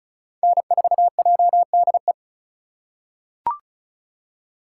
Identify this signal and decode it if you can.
Morse code, heard as a keyed tone, about 36 wpm, 700 Hz. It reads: N4JDE
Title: USA Call Signs - 5 Characters - 2x Character Spacing - 35wpm